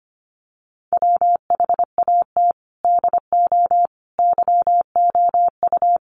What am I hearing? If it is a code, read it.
Morse code, heard as a keyed tone, about 25 wpm, 700 Hz. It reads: WHAT DO YOU
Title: Sets of 5 Words Encore 25wpm